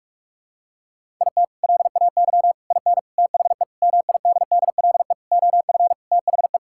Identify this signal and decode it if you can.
Morse code, heard as a keyed tone, about 46 wpm, 700 Hz. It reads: IT LAY IN THE MIDDLE OF THE
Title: Sentences from Top 1000 Words - Review Part 6 - 45wpm